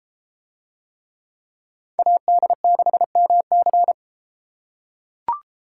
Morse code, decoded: AD6MC

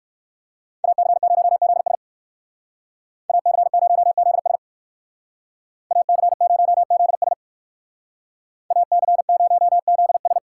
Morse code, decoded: AC0ZS AC0ZS AC0ZS AC0ZS

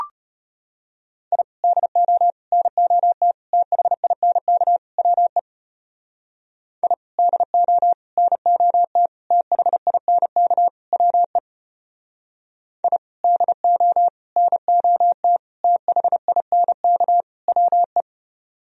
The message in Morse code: I DO NOT THINK WE I DO NOT THINK WE I DO NOT THINK WE

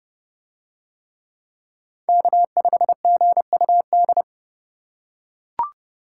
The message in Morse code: K5GUD

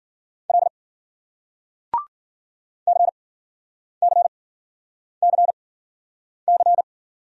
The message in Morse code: L CCCC

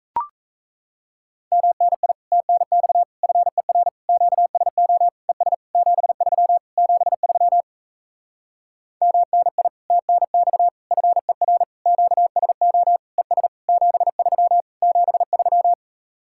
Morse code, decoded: MNI TNX FER QSO ES 73 73 MNI TNX FER QSO ES 73 73

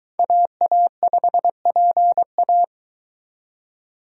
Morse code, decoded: AA5PA

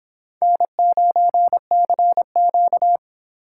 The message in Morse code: N9CQ

